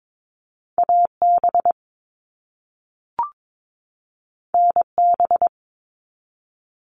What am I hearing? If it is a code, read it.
Morse code, heard as a keyed tone, about 22 wpm, 700 Hz. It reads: AB NB
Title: Canadian Provinces and Territories 22wpm